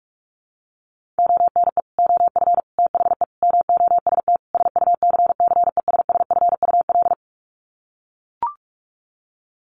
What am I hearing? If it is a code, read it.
Morse code, heard as a keyed tone, about 44 wpm, 700 Hz. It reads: ONE OF THE MOST SUCCESSFUL
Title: Sets of 5 Words Encore2 45wpm